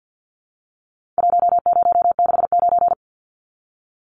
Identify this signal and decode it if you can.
Morse code, heard as a keyed tone, about 48 wpm, 700 Hz. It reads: 1069
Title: Numbers - Contest Style - 4-Digit 50wpm